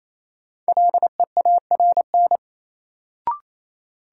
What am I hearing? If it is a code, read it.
Morse code, heard as a keyed tone, about 28 wpm, 700 Hz. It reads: LEARN
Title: Sentences from Top 300 Words - New Word Drill - 28wpm